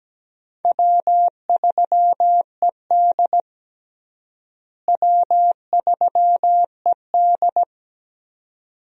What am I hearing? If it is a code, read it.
Morse code, heard as a keyed tone, about 17 wpm, 700 Hz. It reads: W3ED W3ED
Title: USA Call Signs - 4 Characters - Repeated 2x - 17wpm